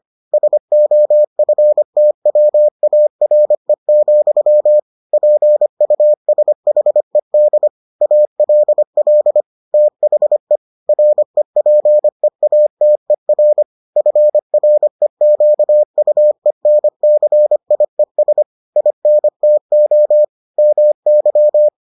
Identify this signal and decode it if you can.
Morse code, heard as a keyed tone, about 25 wpm, 600 Hz. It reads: SOFTWARE, PUSHED ALL THE REPEATER FREQUENCIES INTO MY